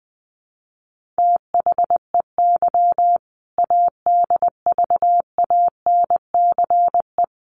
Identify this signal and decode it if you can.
Morse code, heard as a keyed tone, about 20 wpm, 700 Hz. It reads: THEY ADVANCE
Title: Sentences from Top 1400 Words 20wpm